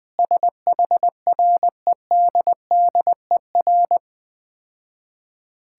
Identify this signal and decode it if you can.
Morse code, heard as a keyed tone, about 20 wpm, 700 Hz. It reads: SHREDDER
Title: Difficult Words - Speed-Racing with Hints - 20wpm